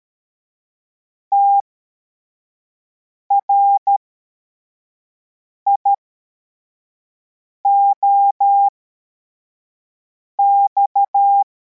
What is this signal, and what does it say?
Morse code, heard as a keyed tone, about 13 wpm, spaced as for 4 wpm, 800 Hz. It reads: TRIOX